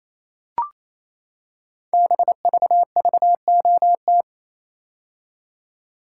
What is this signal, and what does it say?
Morse code, heard as a keyed tone, about 27 wpm, 700 Hz. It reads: BVVOT
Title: Code Groups - 5 Letter:Numbers - 28wpm